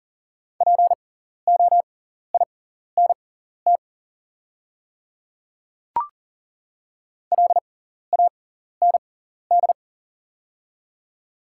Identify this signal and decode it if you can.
Morse code, heard as a keyed tone, about 40 wpm, spaced as for 14 wpm, 700 Hz. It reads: POINT LAND
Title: 6x Character Spacing - Top 200 Words - 40wpm